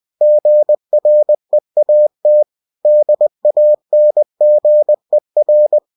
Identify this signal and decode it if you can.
Morse code, heard as a keyed tone, about 20 wpm, 600 Hz. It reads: GREAT DANGER
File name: cw-books / winnie-the-pooh / winnie-20wpm-page-0100